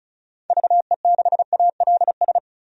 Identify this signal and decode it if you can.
Morse code, heard as a keyed tone, about 34 wpm, 700 Hz. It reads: VE6ALS
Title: DXCC-100 Call Signs - 6 Characters - 35wpm